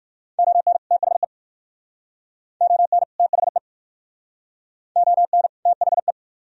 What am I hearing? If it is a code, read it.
Morse code, heard as a keyed tone, about 52 wpm, 700 Hz. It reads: ON THE ON THE ON THE
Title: Speed-Racing - Sets of 2 Words 45wpm